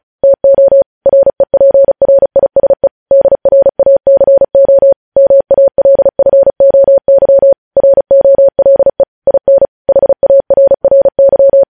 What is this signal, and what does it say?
Morse code, heard as a keyed tone, about 36 wpm, 550 Hz. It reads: TO REPRISE DRACO MALFOY ROLE IN HARRY
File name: podcast / News headlines-1x-35x35